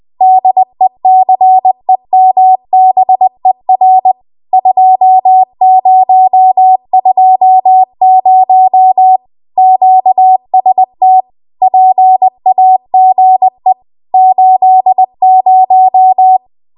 Morse code, heard as a keyed tone, about 20 wpm, 750 Hz. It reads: DECEMBER 2020 QST PAGE 80
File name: Archive / 20 WPM / 20WPM